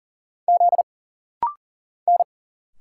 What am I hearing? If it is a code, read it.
Morse code, heard as a keyed tone, about 39 wpm, 700 Hz. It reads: Z N
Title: Single Letter-Number Alt - Rapid-Fire 40wpm